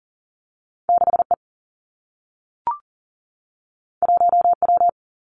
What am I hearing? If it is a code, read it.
Morse code, heard as a keyed tone, about 40 wpm, 700 Hz. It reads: 6 E 1 W
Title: Two Challenging Letter-Number Combinations 40wpm